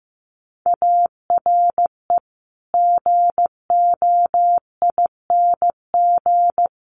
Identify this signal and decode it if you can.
Morse code, heard as a keyed tone, about 15 wpm, 700 Hz. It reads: ARE GOING
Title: Sets of 4 Words 15wpm